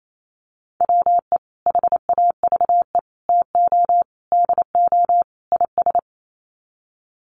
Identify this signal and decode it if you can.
Morse code, heard as a keyed tone, about 28 wpm, 700 Hz. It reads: WE HAVE TO DO IS
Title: Sets of 5 Words Encore 28wpm